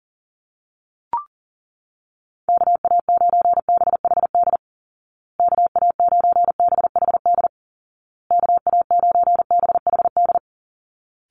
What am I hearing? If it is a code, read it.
Morse code, heard as a keyed tone, about 37 wpm, 700 Hz. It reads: KA9BHD KA9BHD KA9BHD
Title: Call Signs Repeated 2x - 40wpm